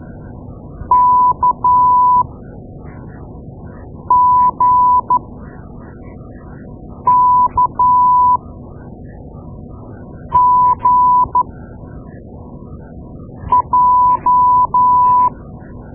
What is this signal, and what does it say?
Morse code, heard as a keyed tone, about 12 wpm, spaced as for 4 wpm, 990 Hz. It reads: KGKGJ
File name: Morse Lessons / Morse Lessons / Tape-2A